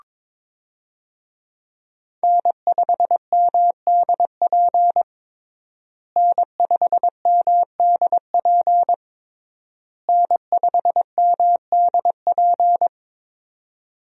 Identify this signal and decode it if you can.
Morse code, heard as a keyed tone, about 22 wpm, 700 Hz. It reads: N5MDP N5MDP N5MDP
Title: USA Call Signs - 5 Characters - Repeated 3x - 22wpm